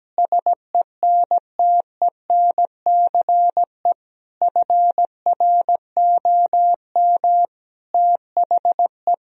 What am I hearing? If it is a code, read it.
Morse code, heard as a keyed tone, about 17 wpm, 700 Hz. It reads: SENTENCE FROM THE
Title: Sentences from Top 1000 Words - Review Part 1 - 17wpm